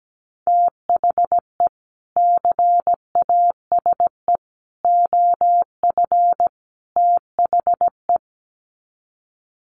Morse code, heard as a keyed tone, about 17 wpm, 700 Hz. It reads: THE CASE OF THE
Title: Sets of 4 Words 17wpm